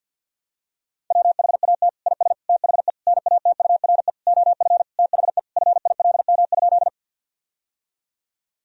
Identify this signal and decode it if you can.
Morse code, heard as a keyed tone, about 50 wpm, 700 Hz. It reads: WHAT IS THE NATURE OF THE FILM?
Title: Sentences from Top 1000 Words - Review Part 3 - 50wpm